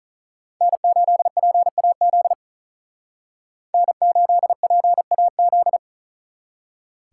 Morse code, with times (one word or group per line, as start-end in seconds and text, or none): 0.6-2.3 N8PAZ
3.7-5.8 N8PAZ